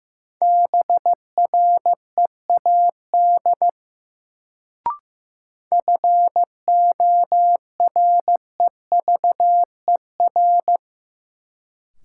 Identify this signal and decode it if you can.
Morse code, heard as a keyed tone, about 15 wpm, 700 Hz. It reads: BREAD FOREVER
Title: Sentences from Top 1600 Words - New Word Drill - 15wpm